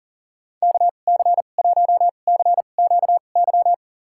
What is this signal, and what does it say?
Morse code, heard as a keyed tone, about 40 wpm, spaced as for 29 wpm, 700 Hz. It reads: KC1CQY
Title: USA Call Signs - 6 Characters - 2x Character Spacing - 40wpm